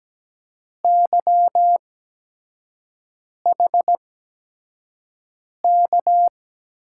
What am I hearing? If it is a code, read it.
Morse code, heard as a keyed tone, about 17 wpm, spaced as for 5 wpm, 700 Hz. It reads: YHK
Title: USA Call Signs - 6 Characters - 8x Character Spacing - 17wpm